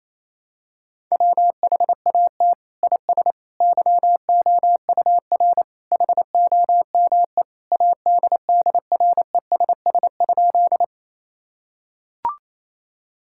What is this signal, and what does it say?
Morse code, heard as a keyed tone, about 27 wpm, 700 Hz. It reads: WHAT IS YOUR HOME ADDRESS?
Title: Sentences from Top 1000 Words - Review Part 6 - 28wpm